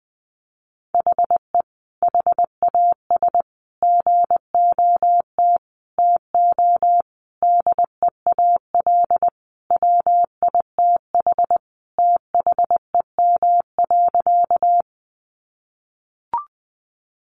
Morse code, heard as a keyed tone, about 20 wpm, 700 Hz. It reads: HE HAS GOT TO DEAL WITH THEM.
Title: Sentences from Top 800 Words 20wpm